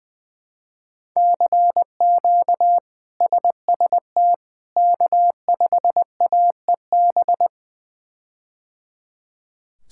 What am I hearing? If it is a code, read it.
Morse code, heard as a keyed tone, about 20 wpm, 700 Hz. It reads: CQ SST K5AEB
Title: K1USN SST QSO Practice - 20wpm